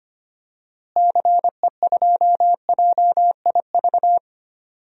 Morse code, heard as a keyed tone, about 25 wpm, 700 Hz. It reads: CE2JIV
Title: DXCC-100 Call Signs - 6 Characters - Repeated 3x - 25wpm